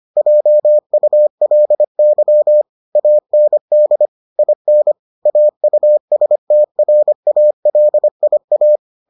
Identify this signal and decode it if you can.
Morse code, heard as a keyed tone, about 25 wpm, 600 Hz. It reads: JULY AND IN AUSTRALIA